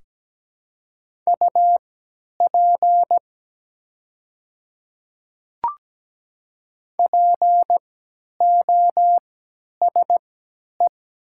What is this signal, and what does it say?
Morse code, heard as a keyed tone, about 17 wpm, spaced as for 10 wpm, 700 Hz. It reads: UP POSE
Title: 3x Character Spacing - Top 500 Words - 17wpm